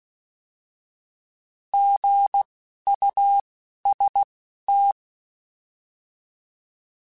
Morse code, heard as a keyed tone, about 16 wpm, spaced as for 11 wpm, 800 Hz. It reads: GUST
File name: Alphabetical / Alphabetical mp3s / Gs @10